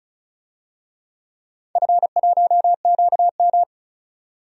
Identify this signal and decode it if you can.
Morse code, heard as a keyed tone, about 34 wpm, 700 Hz. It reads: F1QM
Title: DXCC-100 Call Signs - 4 Characters - 35wpm